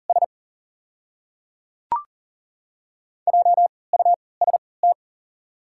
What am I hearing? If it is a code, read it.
Morse code, heard as a keyed tone, about 41 wpm, spaced as for 23 wpm, 690 Hz. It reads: S JUST
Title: 3x Character Spacing - Top 200 Words - 40wpm